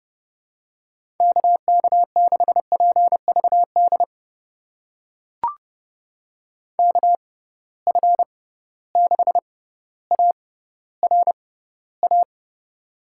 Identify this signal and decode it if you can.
Morse code, heard as a keyed tone, about 31 wpm, spaced as for 16 wpm, 700 Hz. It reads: KK6PVD KF6ARA
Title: USA Call Signs - 6 Characters - 6x Character Spacing - 30wpm